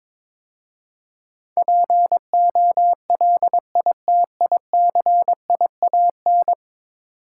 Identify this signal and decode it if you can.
Morse code, heard as a keyed tone, about 22 wpm, 700 Hz. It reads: POLITICIAN
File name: practice / 22wpm / 10 Letter Words - First 3 Letters Given - 22wpm